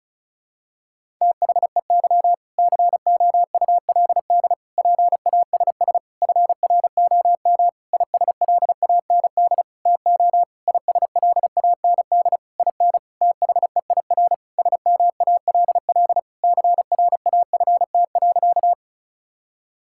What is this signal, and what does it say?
Morse code, heard as a keyed tone, about 34 wpm, 700 Hz. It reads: THEY COULD PASS FROM ISLAND TO ISLAND IN THEIR SMALL CRAFT.
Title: Sentences from Top 1000 Words - Review Part 6 - 35wpm